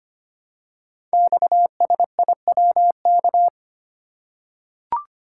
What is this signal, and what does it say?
Morse code, heard as a keyed tone, about 25 wpm, 690 Hz. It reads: XSIWK
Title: Code Groups - 5 Letters - 25wpm